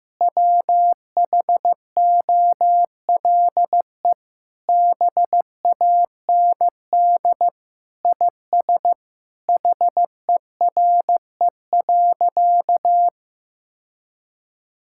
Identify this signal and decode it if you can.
Morse code, heard as a keyed tone, about 15 wpm, 700 Hz. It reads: WHOLE BAND IS HERE.
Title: Sentences from Top 1600 Words 15wpm